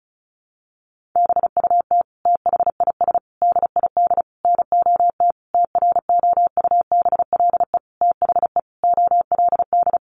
Morse code, heard as a keyed tone, about 34 wpm, 700 Hz. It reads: BUT THIS DID NOT TROUBLE THE OLD
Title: Sentences from Top 1200 Words 35wpm